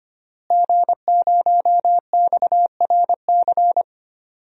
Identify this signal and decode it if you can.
Morse code, heard as a keyed tone, about 25 wpm, 700 Hz. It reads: G0XRC
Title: DXCC-100 Call Signs - 5 Characters - Repeated 2x - 25wpm